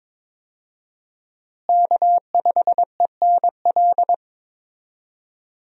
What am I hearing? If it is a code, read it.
Morse code, heard as a keyed tone, about 22 wpm, 700 Hz. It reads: K5ENL
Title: USA Call Signs - 5 Characters - Repeated 3x - 22wpm